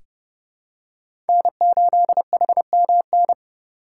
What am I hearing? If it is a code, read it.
Morse code, heard as a keyed tone, about 29 wpm, 700 Hz. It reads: N8HMN
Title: USA Call Signs - 5 Characters - 4x Character Spacing - 30wpm